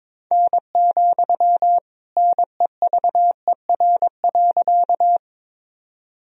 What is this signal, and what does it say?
Morse code, heard as a keyed tone, about 22 wpm, 700 Hz. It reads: N, NEVER.
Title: Sentences from Top 300 Words and Abbreviations 22wpm